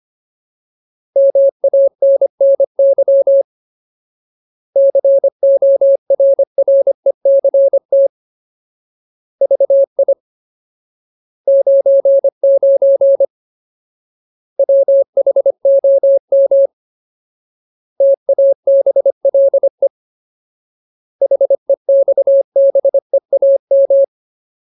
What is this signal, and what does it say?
Morse code, heard as a keyed tone, about 25 wpm, 550 Hz. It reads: MANNY CORRECT VI 99 WHOM TABLE HEXBEAM
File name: Ultimate Word List / 25WPM Ultimate